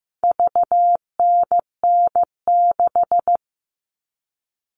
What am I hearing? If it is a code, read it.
Morse code, heard as a keyed tone, about 15 wpm, 700 Hz. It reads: VNN6
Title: All Cut Numbers - 4-Digit 15wpm